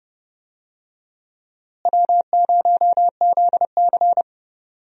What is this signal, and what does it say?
Morse code, heard as a keyed tone, about 30 wpm, 700 Hz. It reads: W0ZC
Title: USA Call Signs - 4 Characters - Repeated 3x - 30wpm